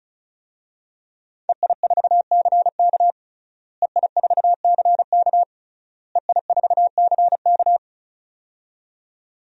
Morse code, heard as a keyed tone, about 34 wpm, 690 Hz. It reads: EI4CK EI4CK EI4CK
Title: DXCC-100 Call Signs - 5 Characters - Repeated 3x - 35wpm